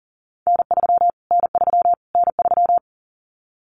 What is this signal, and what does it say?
Morse code, heard as a keyed tone, about 40 wpm, 700 Hz. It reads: N3 N3 N3
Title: Callsign Prefixes for Top 25 DXCC - Repeated 3x - 40wpm